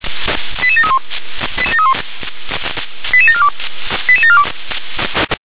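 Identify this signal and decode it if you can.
Five-tone selective call (ZVEI), heard as quick runs of five tones, tones of about 70 ms. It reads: 88731 88742 88742
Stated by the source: ZVEI five-tone selective call received with an SDR.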